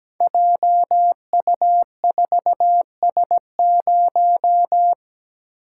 Morse code, heard as a keyed tone, about 17 wpm, 700 Hz. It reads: JU4S0
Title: Code Groups - 5 Letter:Numbers - 17wpm